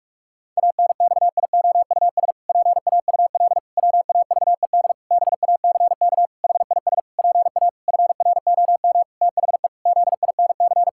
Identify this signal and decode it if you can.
Morse code, heard as a keyed tone, about 46 wpm, 700 Hz. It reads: ANXIOUS PAUL WAVED BACK HIS PA FROM THE ZINC
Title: Pangram Challenge v2 - 45wpm